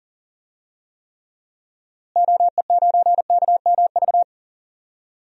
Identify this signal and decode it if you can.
Morse code, heard as a keyed tone, about 40 wpm, 700 Hz. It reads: OE9KMV
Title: DXCC-100 Call Signs - 6 Characters - Repeated 3x - 40wpm